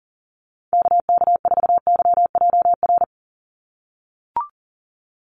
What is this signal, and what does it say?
Morse code, heard as a keyed tone, about 40 wpm, 700 Hz. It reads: KK4YJR
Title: USA Call Signs 40wpm